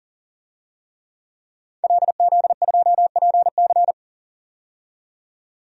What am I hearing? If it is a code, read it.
Morse code, heard as a keyed tone, about 40 wpm, 700 Hz. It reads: LZ2PC
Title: DXCC-100 Call Signs - 5 Characters - 40wpm